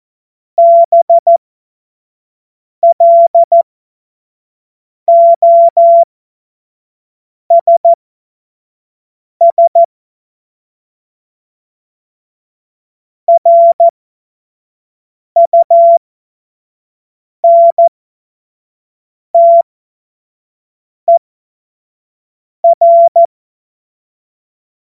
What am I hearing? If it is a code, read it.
Morse code, heard as a keyed tone, about 14 wpm, spaced as for 5 wpm, 680 Hz. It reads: BLOSS RUNTER
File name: Morsekurs / Witze / Witz 11. Juli 2023 25 F70